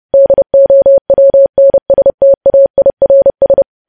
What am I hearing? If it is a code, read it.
Morse code, heard as a keyed tone, about 30 wpm, 550 Hz. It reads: DOWNSTAIRS
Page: news headlines in morse code at 30 wpm